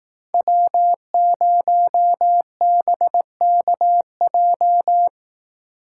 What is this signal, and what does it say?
Morse code, heard as a keyed tone, about 17 wpm, 700 Hz. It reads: W0BKJ
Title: Speed Racing - USA 1x3 Call Signs 15wpm